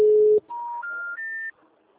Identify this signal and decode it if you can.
Special information tone.